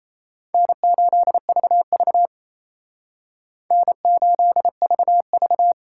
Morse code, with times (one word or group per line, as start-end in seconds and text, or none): 0.5-2.3 N8VV
3.7-5.7 N8VV